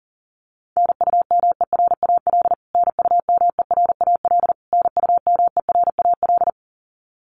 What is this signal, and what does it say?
Morse code, heard as a keyed tone, about 40 wpm, 700 Hz. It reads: NUMERAL NUMERAL NUMERAL
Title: Top 500 Words Repeated 40wpm